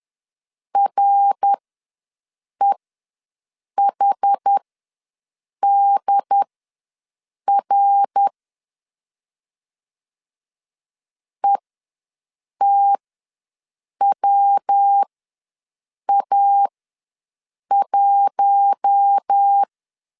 Morse code, heard as a keyed tone, about 10 wpm, spaced as for 6 wpm, 800 Hz. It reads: REHDR ETWA1
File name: LEZIONI / 5 LBVC / 5 L ws 10wpm 3636